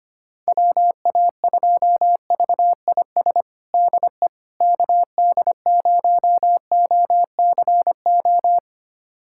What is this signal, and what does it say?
Morse code, heard as a keyed tone, about 25 wpm, 700 Hz. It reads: WA2VIS DE KD0OCO
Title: USA Callsign DE Callsign - 6 Characters - 25wpm